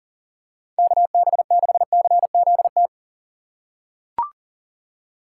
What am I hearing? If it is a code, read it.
Morse code, heard as a keyed tone, about 40 wpm, 700 Hz. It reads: KB6CZT
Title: USA Call Signs - 6 Characters - 3x Character Spacing - 40wpm